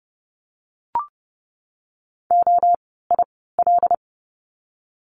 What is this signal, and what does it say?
Morse code, heard as a keyed tone, about 30 wpm, 700 Hz. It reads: O I L
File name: Lesson 16f - Letter c - Words 30wpm